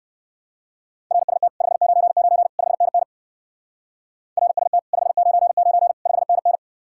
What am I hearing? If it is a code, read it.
Morse code, heard as a keyed tone, about 56 wpm, 700 Hz. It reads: RST 599 5NN RST 599 5NN
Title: Speed-Racing Common QSO Phrases - 45wpm